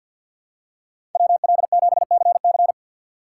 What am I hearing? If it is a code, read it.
Morse code, heard as a keyed tone, about 48 wpm, 700 Hz. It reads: WL7CC
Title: USA Call Signs - 6x Character Spacing - 50wpm